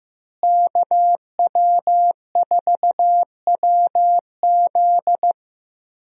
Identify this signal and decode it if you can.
Morse code, heard as a keyed tone, about 15 wpm, 700 Hz. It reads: KW4WZ
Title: USA Call Signs - 5 Characters - Repeated 2x - 15wpm